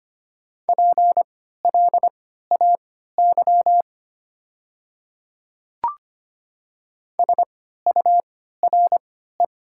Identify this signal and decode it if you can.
Morse code, heard as a keyed tone, about 25 wpm, spaced as for 14 wpm, 700 Hz. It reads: PLAY SURE
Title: 3x Character Spacing - Top 300 Words - 25wpm